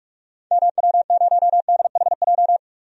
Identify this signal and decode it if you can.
Morse code, heard as a keyed tone, about 47 wpm, 700 Hz. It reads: MW0DHJ
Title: DXCC-100 Call Signs - 6 Characters - 45wpm